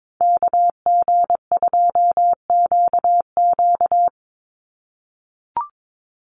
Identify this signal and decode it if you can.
Morse code, heard as a keyed tone, about 22 wpm, 700 Hz. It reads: KG2QQ
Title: USA Call Signs 22wpm